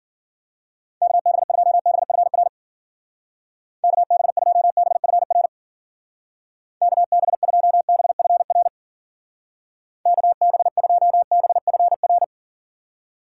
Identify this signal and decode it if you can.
Morse code, heard as a keyed tone, about 48 wpm, 700 Hz. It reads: KB2BFR KB2BFR KB2BFR KB2BFR
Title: Speed Racing - USA 2x3 Call Signs 40wpm